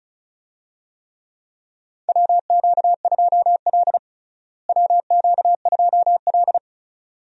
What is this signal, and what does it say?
Morse code, heard as a keyed tone, about 36 wpm, 700 Hz. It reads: WQ2L WQ2L
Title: USA Call Signs - 4 Characters - Repeated 2x - 35wpm